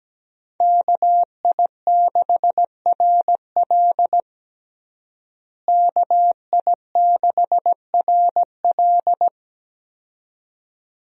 Morse code, heard as a keyed tone, about 17 wpm, 700 Hz. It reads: KI6RL KI6RL
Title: USA Call Signs - 5 Characters - Repeated 2x - 17wpm